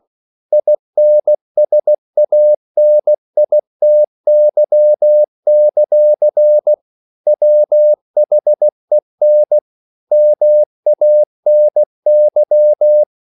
Morse code, heard as a keyed tone, about 16 wpm, 600 Hz. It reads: INSANITY; WHEN MANY
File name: CW / Practice / EnglishQuotations0116